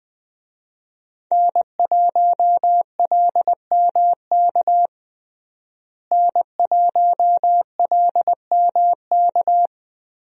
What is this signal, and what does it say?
Morse code, heard as a keyed tone, about 21 wpm, 700 Hz. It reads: N1LMK N1LMK
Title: USA Call Signs - 5 Characters - Repeated 2x - 20wpm